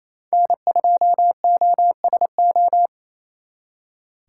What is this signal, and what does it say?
Morse code, heard as a keyed tone, about 37 wpm, 700 Hz. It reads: N2OSO